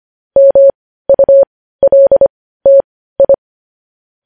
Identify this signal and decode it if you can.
Morse code, heard as a keyed tone, about 25 wpm, spaced as for 15 wpm, 550 Hz. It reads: MULTI